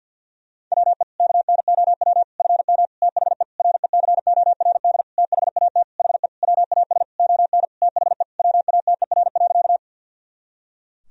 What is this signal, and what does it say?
Morse code, heard as a keyed tone, about 51 wpm, 700 Hz. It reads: WE KNOW FM THE RECORD THAT HE WAS ON THE WATER.
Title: Sentences from Top 500 Words and Abbreviations 50wpm